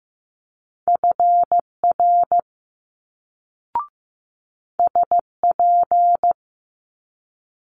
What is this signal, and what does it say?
Morse code, heard as a keyed tone, about 15 wpm, 700 Hz. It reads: FR SP
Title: Multi-letter Phonemes Quick Fire 15wpm